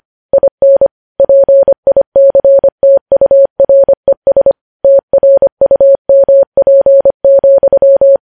Morse code, heard as a keyed tone, about 25 wpm, 550 Hz. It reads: IN PICTURES TRUMP,
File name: podcast / News headlines-1x-25x25